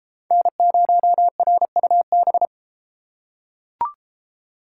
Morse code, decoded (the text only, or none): N0RUB